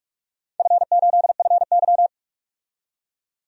Morse code, heard as a keyed tone, about 46 wpm, 700 Hz. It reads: F8FY
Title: DXCC-100 Call Signs - 4 Characters - 45wpm